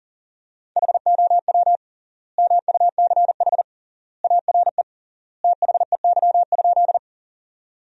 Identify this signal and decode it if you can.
Morse code, heard as a keyed tone, about 37 wpm, 700 Hz. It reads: HOW MUCH ARE THEY?
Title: 3x Word-Spacing - Sentences from Top 200 Words - 40wpm